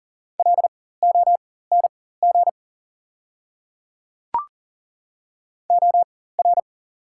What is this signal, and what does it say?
Morse code, heard as a keyed tone, about 40 wpm, spaced as for 19 wpm, 700 Hz. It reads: LONG OR
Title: 4x Character Spacing - Top 100 Words - 40wpm